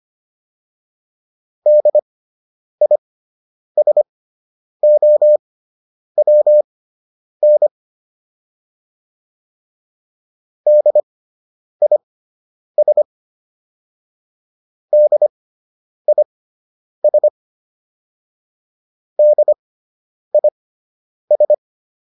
Morse code, heard as a keyed tone, about 25 wpm, spaced as for 9 wpm, 600 Hz. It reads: DISOWN DIS DIS DIS